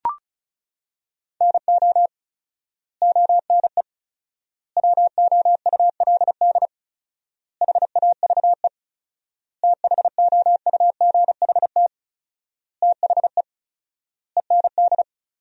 Morse code, heard as a keyed tone, about 34 wpm, 700 Hz. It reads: NO ONE WOULD HAVE THOUGHT THE END
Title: 4x Word-Spacing - Sentences from Top 300 Words - 35wpm